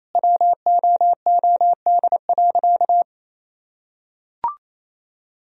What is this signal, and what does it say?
Morse code, heard as a keyed tone, about 27 wpm, 700 Hz. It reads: WOOD.
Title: Sentences from Top 1000 Words - Review Part 3 - 28wpm